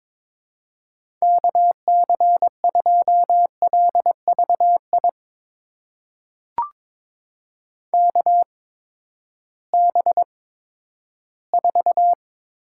Morse code, heard as a keyed tone, about 22 wpm, spaced as for 12 wpm, 700 Hz. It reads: KC2LVI KB4
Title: USA Call Signs - 6 Characters - 8x Character Spacing - 22wpm